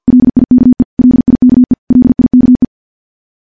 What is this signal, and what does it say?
Morse code, heard as a keyed tone, about 50 wpm, 250 Hz. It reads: LIKE LIKE LIKE
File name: morse / morse-training-50WPM